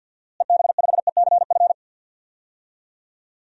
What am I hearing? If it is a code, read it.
Morse code, heard as a keyed tone, about 53 wpm, 700 Hz. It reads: EB5ECF
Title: DXCC-100 Call Signs - 6 Characters - Repeated 2x - 50wpm